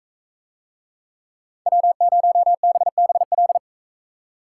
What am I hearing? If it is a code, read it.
Morse code, heard as a keyed tone, about 44 wpm, 700 Hz. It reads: W0BBL